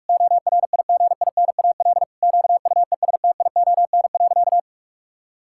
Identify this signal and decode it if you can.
Morse code, heard as a keyed tone, about 41 wpm, 700 Hz. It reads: ORIGINAL QUESTION.
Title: Sentences from Top 1000 Words - Review Part 6 - 45wpm